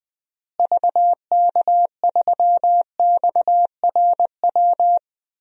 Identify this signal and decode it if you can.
Morse code, heard as a keyed tone, about 21 wpm, 700 Hz. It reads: VK3XRW
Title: DXCC-100 Call Signs - 6 Characters - Repeated 2x - 20wpm